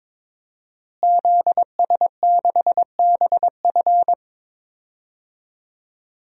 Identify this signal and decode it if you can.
Morse code, heard as a keyed tone, about 22 wpm, 700 Hz. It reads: ZS6BF
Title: DXCC-100 Call Signs - 5 Characters - Repeated 2x - 22wpm